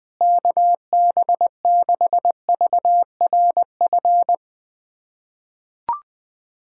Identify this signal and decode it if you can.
Morse code, heard as a keyed tone, about 20 wpm, 700 Hz. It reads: KB6VRF
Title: USA Callsign DE Callsign - 6 Characters - 20wpm